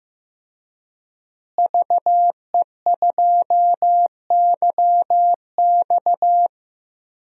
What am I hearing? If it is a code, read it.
Morse code, heard as a keyed tone, about 14 wpm, 700 Hz. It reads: VE2YX
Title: DXCC-100 Call Signs - 5 Characters - Repeated 2x - 15wpm